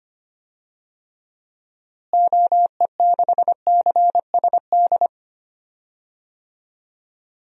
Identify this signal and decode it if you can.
Morse code, heard as a keyed tone, about 25 wpm, 700 Hz. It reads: OE6CSD
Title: DXCC-100 Call Signs - 6 Characters - Repeated 2x - 25wpm